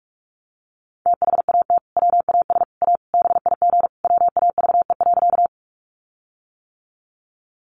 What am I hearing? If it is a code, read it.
Morse code, heard as a keyed tone, about 44 wpm, 700 Hz. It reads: THAT WAS A BIG WAVE.
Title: Sentences from Top 1400 Words 45wpm